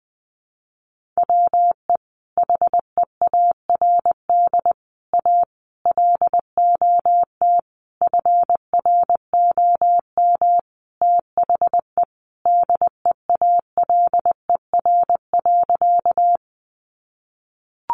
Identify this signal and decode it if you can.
Morse code, heard as a keyed tone, about 20 wpm, 700 Hz. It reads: WE HEARD A LOT FROM THE DEALER.
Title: Sentences from Top 1100 Words 20wpm